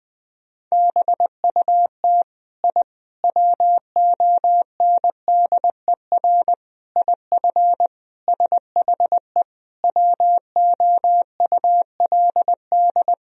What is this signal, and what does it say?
Morse code, heard as a keyed tone, about 20 wpm, 700 Hz. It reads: BUT I WONDER IF SHE WOULD
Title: Sentences from Top 1000 Words - Review Part 3 - 20wpm